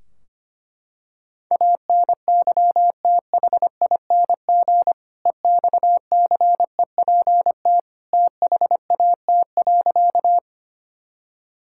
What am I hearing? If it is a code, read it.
Morse code, heard as a keyed tone, about 25 wpm, 700 Hz. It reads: ANYTHING EXCEPT THAT.
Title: Sentences from Top 1600 Words 25wpm